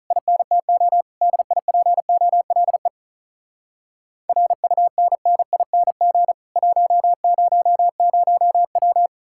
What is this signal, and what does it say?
Morse code, heard as a keyed tone, about 48 wpm, 700 Hz. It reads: INTO DIPOLE RUNNING 100W